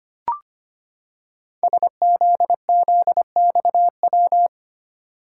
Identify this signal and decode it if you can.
Morse code, heard as a keyed tone, about 25 wpm, 700 Hz. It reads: SZZXW
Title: Code Groups - 5 Letters - 25wpm